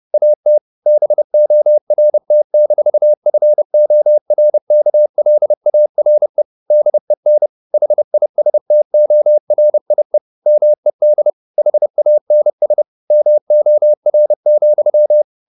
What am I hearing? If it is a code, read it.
Morse code, heard as a keyed tone, about 30 wpm, 600 Hz. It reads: AT BORT-FORKLARE DEN HISTORIE MED HANS MOR,
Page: Ferie 30 wpm